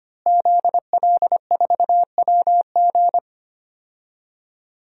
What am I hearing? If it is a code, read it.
Morse code, heard as a keyed tone, about 25 wpm, 700 Hz. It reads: ZL4WG
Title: DXCC-100 Call Signs - 5 Characters - Repeated 3x - 25wpm